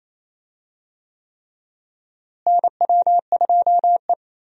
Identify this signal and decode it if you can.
Morse code, heard as a keyed tone, about 27 wpm, 700 Hz. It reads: NW2E
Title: USA Call Signs - 4 Characters - Repeated 3x - 28wpm